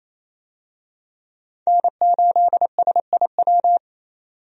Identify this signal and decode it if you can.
Morse code, heard as a keyed tone, about 28 wpm, 700 Hz. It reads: N8SIW
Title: USA Call Signs - 5 Characters - 28wpm